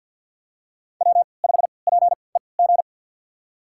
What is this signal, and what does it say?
Morse code, heard as a keyed tone, about 53 wpm, spaced as for 29 wpm, 700 Hz. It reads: W5PEG
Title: USA Call Signs - 3x Character Spacing - 50wpm